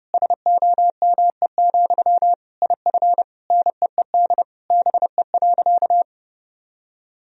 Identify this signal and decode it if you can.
Morse code, heard as a keyed tone, about 30 wpm, 700 Hz. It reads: SOME, IF NEED BE.
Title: Sentences from Top 1000 Words - Review Part 1 - 30wpm